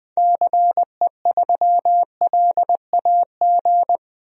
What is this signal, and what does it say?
Morse code, heard as a keyed tone, about 20 wpm, 700 Hz. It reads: CE3LAG
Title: DXCC-100 Call Signs - 6 Characters - 20wpm